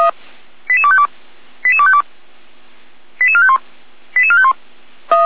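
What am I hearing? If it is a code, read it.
Five-tone selective call (ZVEI), heard as quick runs of five tones, tones of about 70 ms. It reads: 88252 88252 88451 88451
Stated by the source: ZVEI five-tone selective call received by SDR, Feuerwehr Orlamünde (fire brigade).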